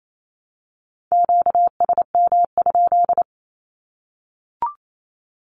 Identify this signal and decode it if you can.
Morse code, heard as a keyed tone, about 28 wpm, 700 Hz. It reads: QSM?
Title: Q-Codes 28wpm